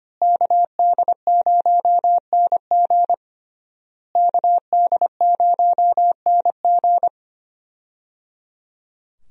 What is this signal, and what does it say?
Morse code, heard as a keyed tone, about 25 wpm, 700 Hz. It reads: KD0NG KD0NG
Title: USA Call Signs - 5 Characters - Repeated 3x - 25wpm